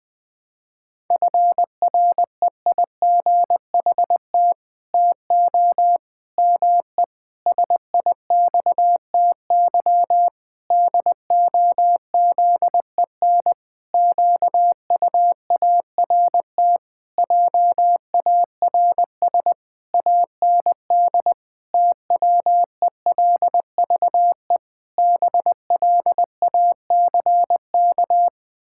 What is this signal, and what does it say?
Morse code, heard as a keyed tone, about 20 wpm, 700 Hz. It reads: FREIGHT TO ME SIXTY DOZEN QUART JARS AND TWELVE BLACK
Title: Pangram Challenge v2 - 20wpm